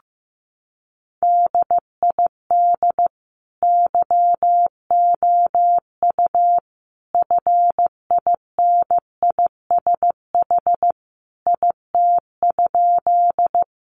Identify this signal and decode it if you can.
Morse code, heard as a keyed tone, about 15 wpm, 700 Hz. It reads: DID YOU FINISH IT?
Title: Sentences from Top 1200 Words 15wpm